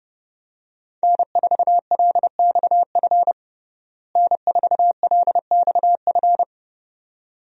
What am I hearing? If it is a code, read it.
Morse code, heard as a keyed tone, about 29 wpm, 700 Hz. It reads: N4LXF N4LXF
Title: USA Call Signs - 5 Characters - Repeated 2x - 30wpm